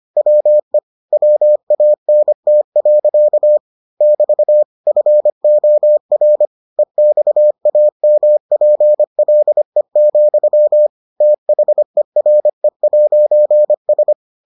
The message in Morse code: WE WANT. <BT> FOR EXAMPLE, THERE'S

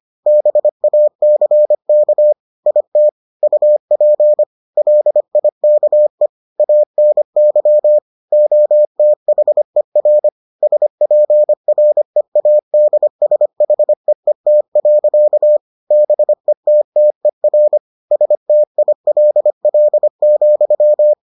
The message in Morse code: BACK IT UP LIKE ANY OTHER SPREADSHEET. BETTER STILL,

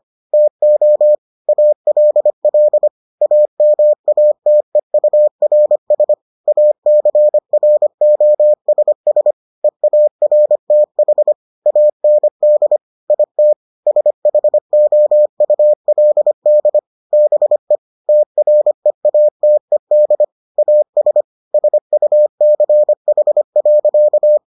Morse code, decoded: TO ALL AMATEURS ACROSS EARTH AND IT SHOULD BE TREATED AS SUCH.

